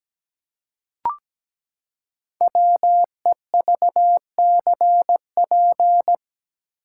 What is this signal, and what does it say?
Morse code, heard as a keyed tone, about 17 wpm, 700 Hz. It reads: WEVCP
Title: Code Groups - 5 Letter:Numbers - 17wpm